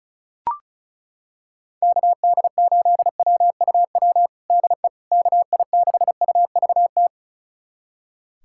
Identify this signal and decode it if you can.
Morse code, heard as a keyed tone, about 36 wpm, 700 Hz. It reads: KD8WUW DE KI6UVT
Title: USA Callsign DE Callsign - 6 Characters - 35wpm